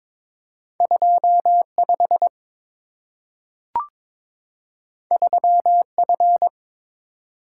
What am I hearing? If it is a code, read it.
Morse code, heard as a keyed tone, about 22 wpm, 700 Hz. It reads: 25 3F
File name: practice / 22wpm / Two Letter-Number Combinations - 22wpm